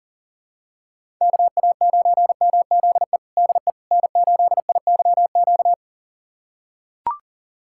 Morse code, heard as a keyed tone, about 40 wpm, 700 Hz. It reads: KA9MZE DE N8IYQ
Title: USA Callsign DE Callsign 40wpm